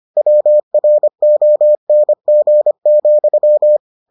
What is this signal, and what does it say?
Morse code, heard as a keyed tone, about 25 wpm, 600 Hz. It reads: WRONG,